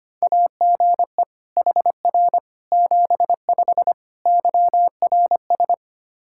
Morse code, heard as a keyed tone, about 25 wpm, 700 Hz. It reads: AGE HR 75 YRS
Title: K1USN SST QSO Practice 25wpm